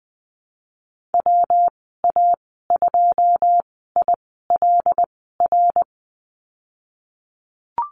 Morse code, heard as a keyed tone, about 20 wpm, spaced as for 15 wpm, 700 Hz. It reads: WA2ILR
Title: 161 Lesson 26g - Letter g - Call Signs 20wpm